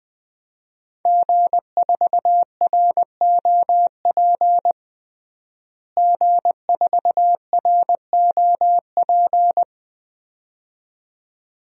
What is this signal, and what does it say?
Morse code, heard as a keyed tone, about 20 wpm, 700 Hz. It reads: G4ROP G4ROP
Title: DXCC-100 Call Signs - 5 Characters - Repeated 2x - 20wpm